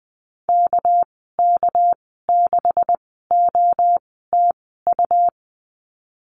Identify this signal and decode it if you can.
Morse code, heard as a keyed tone, about 20 wpm, spaced as for 14 wpm, 700 Hz. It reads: KK6OTU
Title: 245 Lesson 39g - Letter z - Call Signs 20wpm